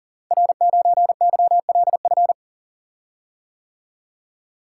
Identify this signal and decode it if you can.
Morse code, heard as a keyed tone, about 40 wpm, 700 Hz. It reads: R9YLF
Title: Code Groups - 5 Letter:Numbers - 40wpm